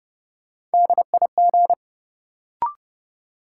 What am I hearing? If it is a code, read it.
Morse code, heard as a keyed tone, about 31 wpm, 700 Hz. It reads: DIG
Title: Sentences from Top 1600 Words - New Word Drill - 30wpm